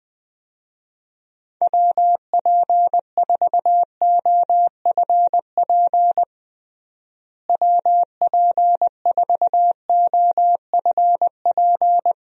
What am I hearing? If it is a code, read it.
Morse code, heard as a keyed tone, about 20 wpm, 700 Hz. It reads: WP4OFP WP4OFP
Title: USA Call Signs - 6 Characters - Repeated 2x - 20wpm